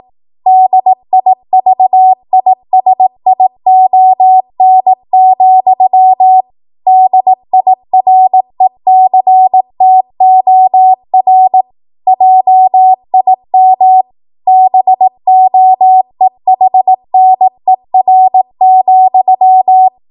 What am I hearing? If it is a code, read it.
Morse code, heard as a keyed tone, about 18 wpm, 750 Hz. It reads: DIVISION, DIRECTOR JIM BOEHNER,